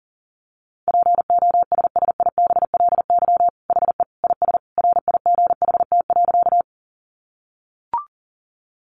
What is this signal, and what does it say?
Morse code, heard as a keyed tone, about 40 wpm, 700 Hz. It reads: POSSIBLY HE IS RIGHT.
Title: Sentences from Top 1300 Words 40wpm